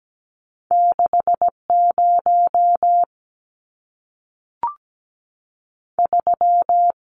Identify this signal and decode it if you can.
Morse code, heard as a keyed tone, about 17 wpm, 700 Hz. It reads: 6 0 3
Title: Numbers - 2-Digit 17wpm